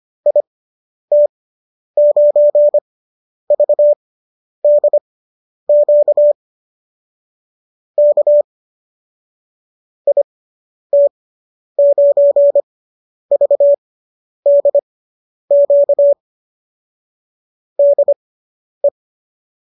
Morse code, heard as a keyed tone, about 25 wpm, spaced as for 10 wpm, 590 Hz. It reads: IT9VDQ K IT9VDQ DE